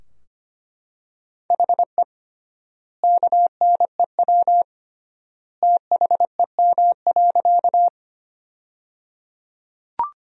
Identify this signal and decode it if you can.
Morse code, heard as a keyed tone, about 25 wpm, 700 Hz. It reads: HE KNEW THEM.
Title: 3x Word-Spacing - Sentences from Top 400 Words - 25wpm